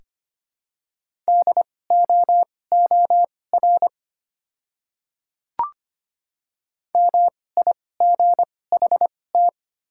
Morse code, heard as a keyed tone, about 25 wpm, spaced as for 18 wpm, 700 Hz. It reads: DOOR MIGHT
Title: 2x Character Spacing - Top 300 Words - 25wpm